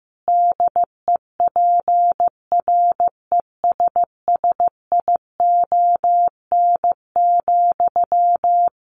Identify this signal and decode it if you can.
Morse code, heard as a keyed tone, about 15 wpm, 700 Hz. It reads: DEPRESSION,
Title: A Sound Of Thunder 15wpm